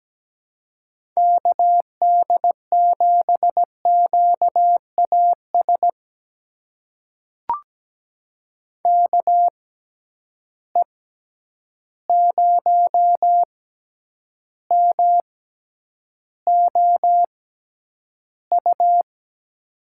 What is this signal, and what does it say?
Morse code, heard as a keyed tone, about 17 wpm, spaced as for 9 wpm, 700 Hz. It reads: KD7QAS KE0MOU
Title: USA Call Signs - 6 Characters - 6x Character Spacing - 17wpm